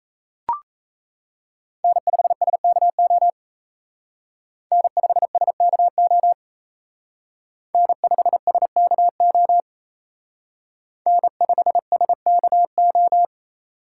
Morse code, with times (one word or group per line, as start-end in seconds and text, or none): 1.8-3.3 N5SKO
4.7-6.3 N5SKO
7.7-9.6 N5SKO
11.1-13.2 N5SKO